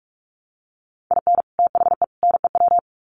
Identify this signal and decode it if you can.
Morse code, heard as a keyed tone, about 44 wpm, 700 Hz. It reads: IN THE NEW
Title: Sets of 3 Words 45wpm